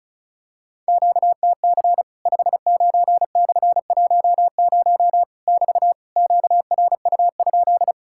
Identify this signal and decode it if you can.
Morse code, heard as a keyed tone, about 34 wpm, 700 Hz. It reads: QTC 59/10 <BT> QRU?
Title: Worked All Europe DX Contest - QTC Exchange - 35wpm